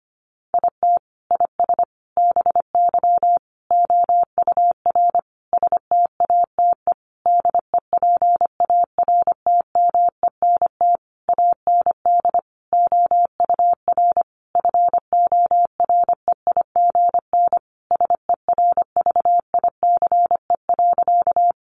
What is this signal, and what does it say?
Morse code, heard as a keyed tone, about 25 wpm, 700 Hz. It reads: IT IS BY OUR STATE DEPARTMENT AND OUR FOREIGN SERVICE.
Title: Sentences from Top 1400 Words 25wpm